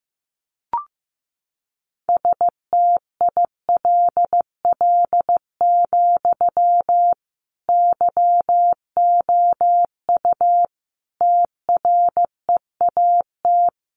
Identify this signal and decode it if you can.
Morse code, heard as a keyed tone, about 15 wpm, 700 Hz. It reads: STILL, YOU TREAT
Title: Sentences from Top 1300 Words 15wpm